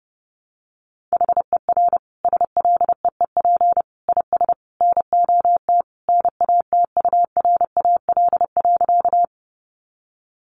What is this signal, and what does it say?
Morse code, heard as a keyed tone, about 30 wpm, 700 Hz. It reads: HER SLEEP IS NOT NATURAL.
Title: Sentences from Top 700 Words 30wpm